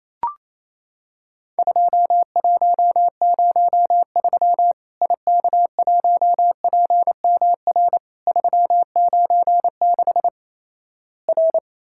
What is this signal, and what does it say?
Morse code, heard as a keyed tone, about 28 wpm, 700 Hz. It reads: 2103 IK1PMR 396 R
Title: Worked All Europe DX Contest - QTC Exchange - 28wpm